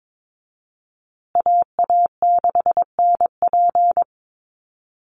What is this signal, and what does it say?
Morse code, heard as a keyed tone, about 22 wpm, 700 Hz. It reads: AA6NP
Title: USA Call Signs Repeated 3x 22wpm